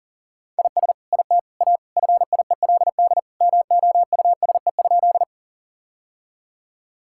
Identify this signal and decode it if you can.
Morse code, heard as a keyed tone, about 40 wpm, 700 Hz. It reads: IS IT A FIELD MOUSE?
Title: Sentences from Top 1000 Words - Review Part 5 - 40wpm